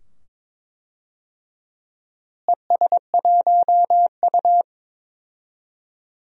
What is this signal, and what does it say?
Morse code, heard as a keyed tone, about 22 wpm, 700 Hz. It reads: ES1U
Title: DXCC-100 Call Signs - 4 Characters - Repeated 2x - 22wpm